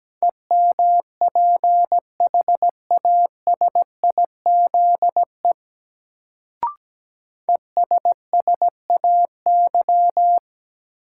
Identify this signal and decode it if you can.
Morse code, heard as a keyed tone, about 17 wpm, 700 Hz. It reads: EMPHASIZE ESSAY
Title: Sentences from Top 1500 Words - New Word Drill - 17wpm